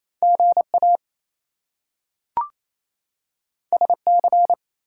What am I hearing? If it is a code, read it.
Morse code, heard as a keyed tone, about 28 wpm, 700 Hz. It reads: GA SC
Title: US States and Canadian Provinces 28wpm